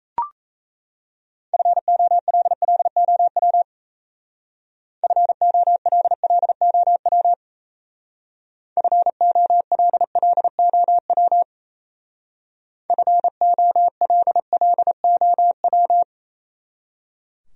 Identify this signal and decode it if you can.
Morse code, heard as a keyed tone, about 33 wpm, 700 Hz. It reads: FOLLOW FOLLOW FOLLOW FOLLOW
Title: Speed-Racing Top 200 Words 28wpm